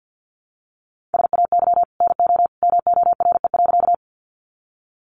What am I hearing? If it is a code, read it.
Morse code, heard as a keyed tone, about 50 wpm, 700 Hz. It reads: SAY NO MORE.
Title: Sentences from Top 200 Words 50wpm